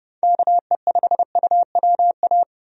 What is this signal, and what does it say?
Morse code, heard as a keyed tone, about 32 wpm, 700 Hz. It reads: KE5UWA
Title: USA Call Signs - 6 Characters - 30wpm